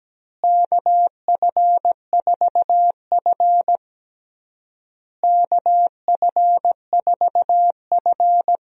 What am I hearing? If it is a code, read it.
Morse code, heard as a keyed tone, about 17 wpm, 700 Hz. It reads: KF4F KF4F
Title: USA Call Signs - 4 Characters - Repeated 3x - 17wpm